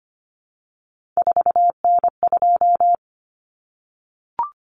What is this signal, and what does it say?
Morse code, heard as a keyed tone, about 25 wpm, 700 Hz. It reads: 4N2
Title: Common Cut Numbers - 3-Digit 25wpm